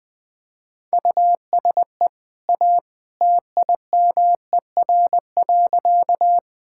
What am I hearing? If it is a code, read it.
Morse code, heard as a keyed tone, about 20 wpm, 700 Hz. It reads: USE A TIMER.
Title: Easy Sentences with Hints - 20wpm